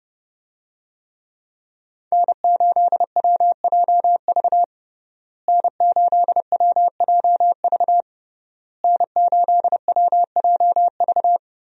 Morse code, heard as a keyed tone, about 30 wpm, 700 Hz. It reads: N8WJV N8WJV N8WJV
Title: USA Call Signs - 5 Characters - Repeated 3x - 30wpm